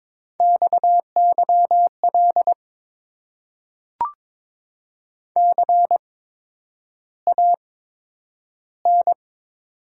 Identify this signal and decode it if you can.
Morse code, heard as a keyed tone, about 22 wpm, spaced as for 9 wpm, 690 Hz. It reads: XYL CAN
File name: Top 90 QSO Elements - 8x Character Spacing - 22wpm